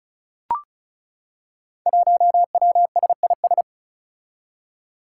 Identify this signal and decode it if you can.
Morse code, heard as a keyed tone, about 36 wpm, 700 Hz. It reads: 1WSIS
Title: Five Challenging Letter-Number Combinations 35wpm